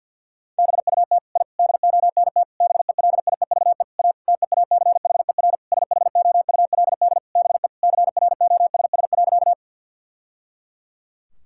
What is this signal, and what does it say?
Morse code, heard as a keyed tone, about 50 wpm, 700 Hz. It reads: BUT I DONT BELIEVE A TEACHER SHOULD BE CROSS.
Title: Sentences from Top 1000 Words - Review Part 6 - 50wpm